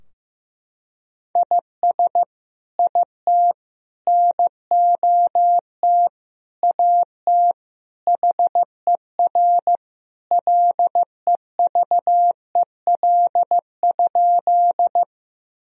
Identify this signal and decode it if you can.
Morse code, heard as a keyed tone, about 15 wpm, 700 Hz. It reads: IS IT NOT AT HER LEVEL?
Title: Sentences from Top 1000 Words - Review Part 4 - 15wpm